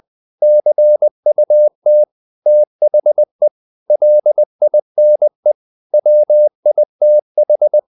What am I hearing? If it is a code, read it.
Morse code, heard as a keyed tone, about 20 wpm, 600 Hz. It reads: CUT THE LINE WITH
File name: the old man and the sea / the old man and the sea part 1 20wpm